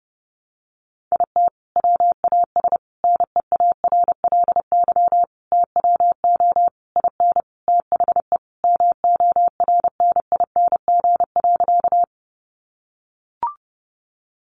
Morse code, heard as a keyed tone, about 30 wpm, 700 Hz. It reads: IT WAS NEARLY TWO IN THE MORNING.
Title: Sentences from Top 1300 Words 30wpm